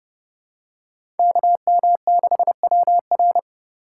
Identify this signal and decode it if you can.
Morse code, heard as a keyed tone, about 30 wpm, 700 Hz. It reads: KM6WR
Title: USA Call Signs - 3x Character Spacing - 30wpm